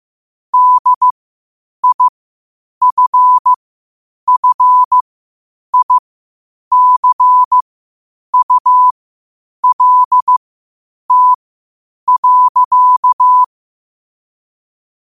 Morse code, heard as a keyed tone, about 15 wpm, spaced as for 8 wpm, 1000 Hz. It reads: DIFFICULT.
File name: qotdfiles / 8wpm